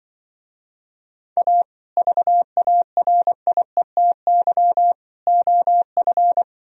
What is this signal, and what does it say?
Morse code, heard as a keyed tone, about 24 wpm, 700 Hz. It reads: A VARIETY OF
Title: Speed-Racing - Sets of 3 Words 20wpm